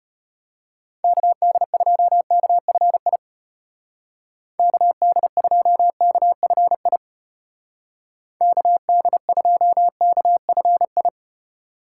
Morse code, heard as a keyed tone, about 34 wpm, 700 Hz. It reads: KD2KFI KD2KFI KD2KFI